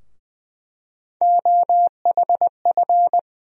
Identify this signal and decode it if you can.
Morse code, heard as a keyed tone, about 21 wpm, 700 Hz. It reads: OHF
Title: USA Call Sign Postfixes - 3 Letters 20wpm